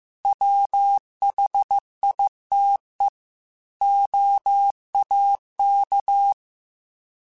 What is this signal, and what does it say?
Morse code, heard as a keyed tone, about 15 wpm, 800 Hz. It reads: WHITE OAK